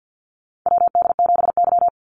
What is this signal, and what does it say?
Morse code, heard as a keyed tone, about 48 wpm, 700 Hz. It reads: WD7Y
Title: NA SOTA Chaser Callsigns 50wpm